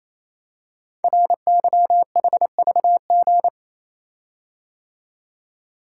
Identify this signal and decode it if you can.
Morse code, heard as a keyed tone, about 27 wpm, 700 Hz. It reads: RYHVG
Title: Code Groups - 5 Letters - 28wpm